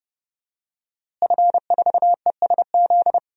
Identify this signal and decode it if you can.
Morse code, heard as a keyed tone, about 31 wpm, 700 Hz. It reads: F4ESZ
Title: DXCC-100 Call Signs - 5 Characters - Repeated 2x - 30wpm